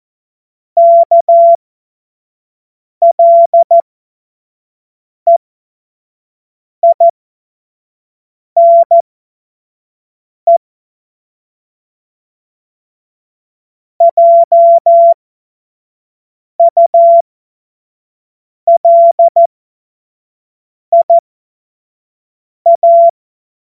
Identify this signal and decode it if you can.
Morse code, heard as a keyed tone, about 14 wpm, spaced as for 5 wpm, 680 Hz. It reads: KLEINE JULIA
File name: Morsekurs / Witze / Witz 06. Dezember 2022 25 F70